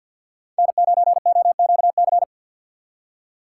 Morse code, heard as a keyed tone, about 67 wpm, spaced as for 58 wpm, 700 Hz. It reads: N9OQC
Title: Speed Racing - USA Call Signs 50wpm